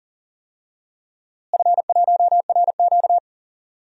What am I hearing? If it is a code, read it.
Morse code, heard as a keyed tone, about 40 wpm, 700 Hz. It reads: F1RQ
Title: DXCC-100 Call Signs - 4 Characters - Repeated 2x - 40wpm